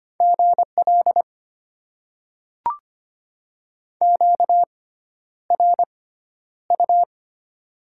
Morse code, heard as a keyed tone, about 25 wpm, spaced as for 11 wpm, 690 Hz. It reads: GL QRU
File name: Top 90 QSO Elements - 6x Character Spacing - 25wpm